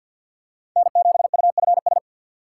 Morse code, heard as a keyed tone, about 48 wpm, 700 Hz. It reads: N7UFS
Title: USA Call Signs - 5 Characters - 4x Character Spacing - 50wpm